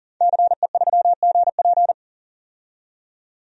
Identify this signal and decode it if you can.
Morse code, heard as a keyed tone, about 40 wpm, 700 Hz. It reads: CE3GP
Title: DXCC-100 Call Signs - 5 Characters - 40wpm